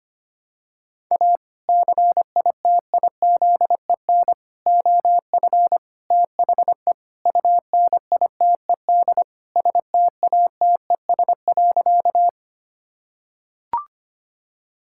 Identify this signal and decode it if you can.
Morse code, heard as a keyed tone, about 25 wpm, 700 Hz. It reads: A CITIZEN OF THE UNITED STATES.
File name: practice / 25wpm / Sentences with 3-Word Phrases - 25wpm